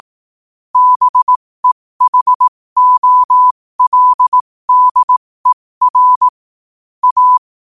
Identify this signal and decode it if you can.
Morse code, heard as a keyed tone, about 18 wpm, spaced as for 15 wpm, 1000 Hz. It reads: BEHOLDER A
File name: qotdfiles / 15wpm